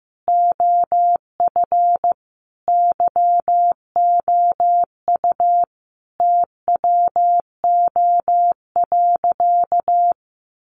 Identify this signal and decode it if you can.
Morse code, heard as a keyed tone, about 15 wpm, 700 Hz. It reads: OF YOU TWO.
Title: Sentences from Top 1200 Words 15wpm